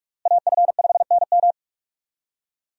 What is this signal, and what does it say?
Morse code, heard as a keyed tone, about 46 wpm, 700 Hz. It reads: AF5NM
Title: USA Call Signs - 5 Characters - Repeated 2x - 45wpm